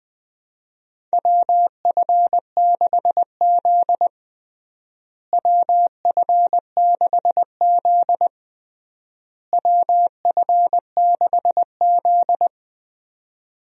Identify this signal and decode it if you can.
Morse code, heard as a keyed tone, about 20 wpm, 700 Hz. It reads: WF6Z WF6Z WF6Z
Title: USA Call Signs - 4 Characters - Repeated 3x - 20wpm